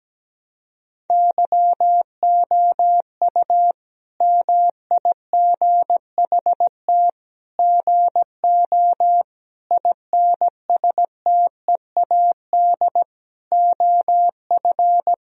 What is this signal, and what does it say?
Morse code, heard as a keyed tone, about 17 wpm, 700 Hz. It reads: YOU MIGHT GO INSTEAD OF
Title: Sentences from Top 1000 Words - Review Part 4 - 17wpm